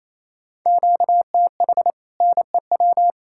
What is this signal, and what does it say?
Morse code, heard as a keyed tone, about 37 wpm, 700 Hz. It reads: QTH NEW